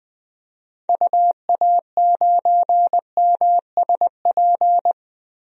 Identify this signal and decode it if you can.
Morse code, heard as a keyed tone, about 20 wpm, 700 Hz. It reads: UA9MSP
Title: DXCC-100 Call Signs - 6 Characters - 20wpm